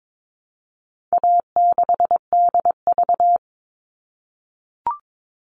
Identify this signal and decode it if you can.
Morse code, heard as a keyed tone, about 22 wpm, 700 Hz. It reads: A6DV
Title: All Cut Numbers - 4-Digit 22wpm